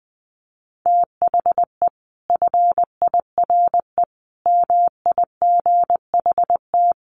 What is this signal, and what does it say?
Morse code, heard as a keyed tone, about 20 wpm, 700 Hz. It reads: THE FIRE MIGHT
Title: Sentences from Top 1300 Words 20wpm